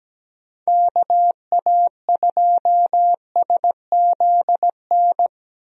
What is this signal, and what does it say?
Morse code, heard as a keyed tone, about 17 wpm, 700 Hz. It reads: KA2SZN
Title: USA Call Signs - 6 Characters - Repeated 3x - 17wpm